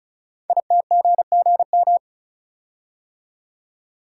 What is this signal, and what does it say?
Morse code, heard as a keyed tone, about 36 wpm, 690 Hz. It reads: ITGGM
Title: Code Groups - 5 Letters - 35wpm